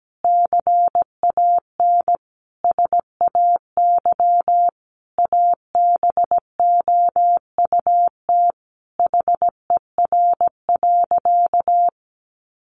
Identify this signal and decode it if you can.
Morse code, heard as a keyed tone, about 17 wpm, 700 Hz. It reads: CAN SAY ABOUT HER.
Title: Sentences from Top 200 Words 17wpm